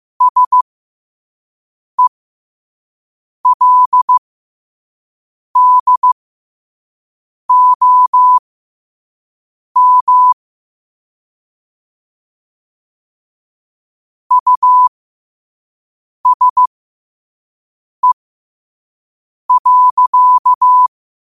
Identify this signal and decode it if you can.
Morse code, heard as a keyed tone, about 15 wpm, spaced as for 5 wpm, 1000 Hz. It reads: SELDOM USE.